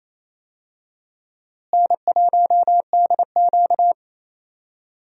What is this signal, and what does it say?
Morse code, heard as a keyed tone, about 28 wpm, 700 Hz. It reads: N1DQ
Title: USA Call Signs - 4 Characters - Repeated 2x - 28wpm